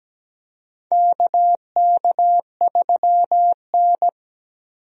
Morse code, heard as a keyed tone, about 17 wpm, 700 Hz. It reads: KK3N
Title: USA Call Signs - 4 Characters - 17wpm